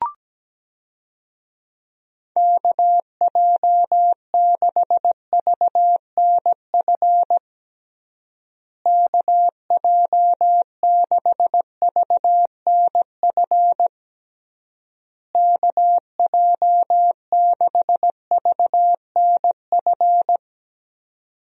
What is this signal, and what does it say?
Morse code, heard as a keyed tone, about 17 wpm, 700 Hz. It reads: KJ6VNF KJ6VNF KJ6VNF
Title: USA Call Signs - 6 Characters - Repeated 3x - 17wpm